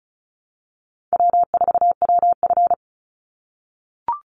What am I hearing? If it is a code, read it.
Morse code, heard as a keyed tone, about 34 wpm, 700 Hz. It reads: W4WF
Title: Call Signs - 35wpm